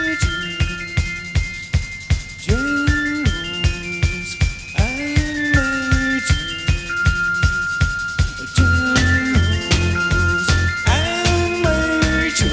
Dial-up modem handshake.